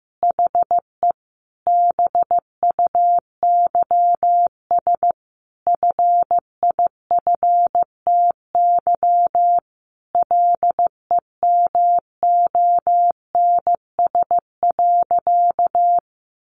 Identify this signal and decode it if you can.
Morse code, heard as a keyed tone, about 15 wpm, 700 Hz. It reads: HE BUYS FIFTY LEMONS.
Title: Additional Easy Phrases 15wpm